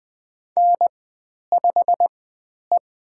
Morse code, heard as a keyed tone, about 20 wpm, spaced as for 10 wpm, 700 Hz. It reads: N5E
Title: 109 Lesson 18g - Letter m - Call Signs 20wpm